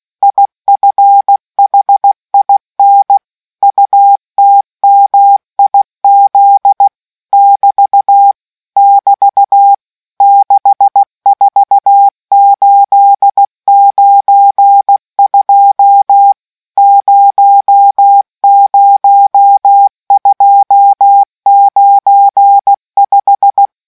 Morse code, heard as a keyed tone, about 16 wpm, 800 Hz. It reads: IFHIN UTMIZ <BT> <BT> 64892 00295